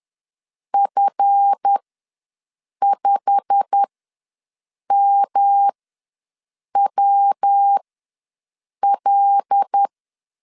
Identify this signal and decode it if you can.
Morse code, heard as a keyed tone, about 10 wpm, spaced as for 6 wpm, 800 Hz. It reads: F5MWL
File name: LEZIONI / 9 28 ws 10wpm 4124